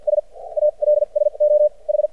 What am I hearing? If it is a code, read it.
Morse code, heard as a keyed tone, about 50 wpm, spaced as for 39 wpm, 600 Hz. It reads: S ALSO H
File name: archived / CRSnet